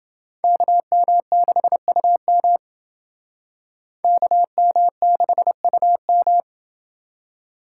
Morse code, heard as a keyed tone, about 29 wpm, 700 Hz. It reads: KM6UM KM6UM